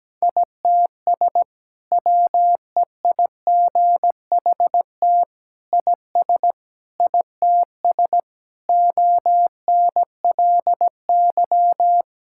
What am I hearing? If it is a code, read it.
Morse code, heard as a keyed tone, about 17 wpm, 700 Hz. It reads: ITS WEIGHT IS ITS ONLY
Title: Sentences from Top 1000 Words - Review Part 6 - 17wpm